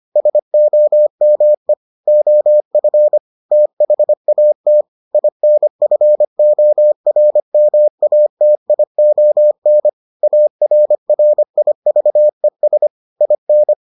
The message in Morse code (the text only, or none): SOME OF THAT INFORMATION ARRIVES IN